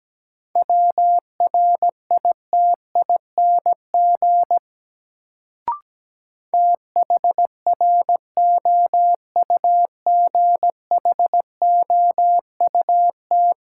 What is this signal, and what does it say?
Morse code, heard as a keyed tone, about 17 wpm, 700 Hz. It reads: WRITING THROUGHOUT
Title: Sentences from Top 800 Words - New Word Drill - 17wpm